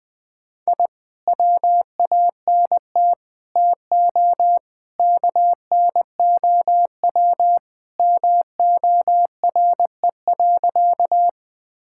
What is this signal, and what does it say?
Morse code, heard as a keyed tone, about 20 wpm, 700 Hz. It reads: I WANT TO KNOW MORE.
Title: Sentences with 2-Word Phrases - 20wpm